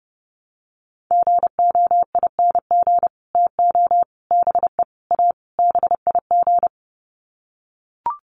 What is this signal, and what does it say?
Morse code, heard as a keyed tone, about 30 wpm, 700 Hz. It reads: GOING TO BE A BIG
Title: Sets of 5 Words Encore2 30wpm